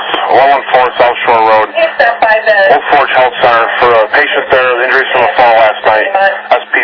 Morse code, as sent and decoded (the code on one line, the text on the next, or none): none